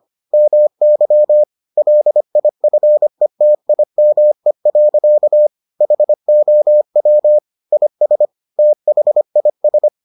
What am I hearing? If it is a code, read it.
Morse code, heard as a keyed tone, about 25 wpm, 600 Hz. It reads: MY LIFETIME. HOW IS THIS